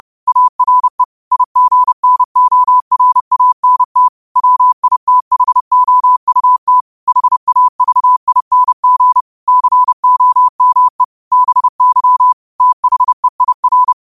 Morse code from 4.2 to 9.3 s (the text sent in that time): WITHOUT HAVING